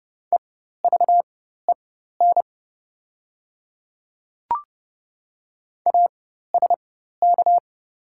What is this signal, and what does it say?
Morse code, heard as a keyed tone, about 30 wpm, spaced as for 14 wpm, 700 Hz. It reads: EVEN ASK
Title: 4x Character Spacing - Top 300 Words - 30wpm